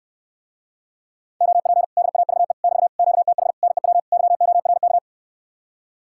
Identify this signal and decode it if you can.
Morse code, heard as a keyed tone, about 69 wpm, 700 Hz. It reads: OP DAVE <BT> QTH NEW YORK